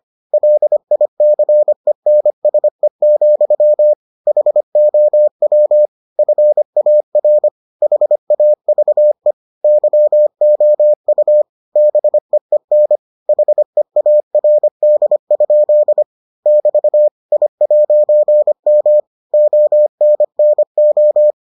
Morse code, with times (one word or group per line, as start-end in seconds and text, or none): 0.3-3.9 LICENSE,
4.3-5.9 HOW
6.2-7.5 FAR
7.8-9.3 HAVE
9.6-11.4 YOU
11.8-13.0 BEEN
13.3-16.0 HEARD?
16.5-17.1 <BT>
17.3-19.0 I'M
19.3-21.3 ONNO